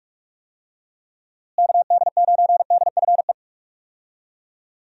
Morse code, heard as a keyed tone, about 46 wpm, 700 Hz. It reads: KD9DFE
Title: USA Call Signs - 6 Characters - Repeated 2x - 45wpm